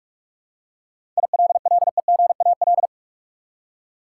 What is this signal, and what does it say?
Morse code, heard as a keyed tone, about 44 wpm, 700 Hz. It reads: ILLEGAL
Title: Seven Letter Words - 45wpm